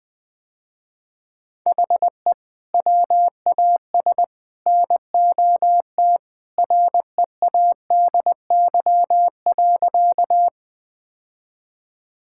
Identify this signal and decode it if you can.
Morse code, heard as a keyed tone, about 20 wpm, 700 Hz. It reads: HE WAS NOT READY.
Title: Sentences with 3-Word Phrases - Spoken Beginning - 20wpm